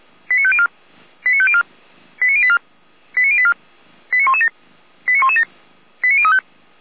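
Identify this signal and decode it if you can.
Five-tone selective call (ZVEI), heard as quick runs of five tones, tones of about 70 ms. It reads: 89504 89504 89084 89084 89117 89117 89035
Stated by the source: ZVEI five-tone selective call received with an SDR.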